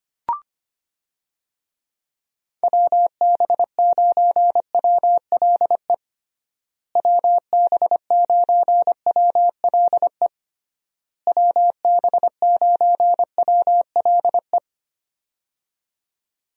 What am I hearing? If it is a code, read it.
Morse code, heard as a keyed tone, about 25 wpm, 700 Hz. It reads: WB9WLE WB9WLE WB9WLE
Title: USA Call Signs - 6 Characters - Repeated 3x - 25wpm